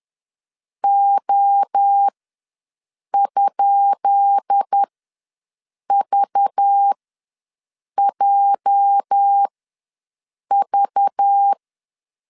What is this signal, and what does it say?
Morse code, heard as a keyed tone, about 10 wpm, spaced as for 6 wpm, 800 Hz. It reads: O?VJV
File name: LEZIONI / 10 SIGNS / 10B INT ws 10wpm 4134